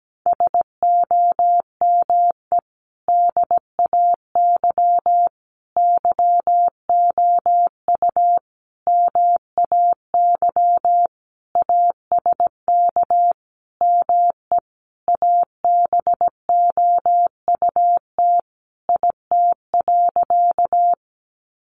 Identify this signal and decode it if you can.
Morse code, heard as a keyed tone, about 17 wpm, 700 Hz. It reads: SOME DAY YOU MAY ASK ME ABOUT IT.
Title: Sentences from Top 200 Words 17wpm